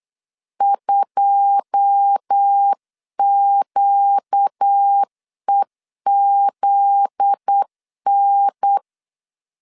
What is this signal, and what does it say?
Morse code, heard as a keyed tone, about 8 wpm, 800 Hz. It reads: 2QEZN
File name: CW / DET / 1 9 1500C S A 8wpm 5637